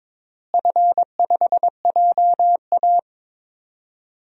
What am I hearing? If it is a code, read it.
Morse code, heard as a keyed tone, about 22 wpm, 700 Hz. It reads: F5JA
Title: DXCC-100 Call Signs - 4 Characters - Repeated 3x - 22wpm